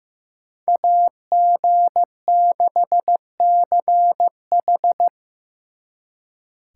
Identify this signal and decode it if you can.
Morse code, heard as a keyed tone, about 18 wpm, 700 Hz. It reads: AG6CH
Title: Speed Racing - USA Call Signs 15wpm